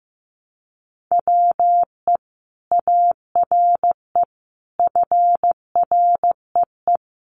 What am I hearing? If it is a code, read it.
Morse code, heard as a keyed tone, about 15 wpm, 700 Hz. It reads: WE ARE FREE
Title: Sentences from Top 500 Words 15wpm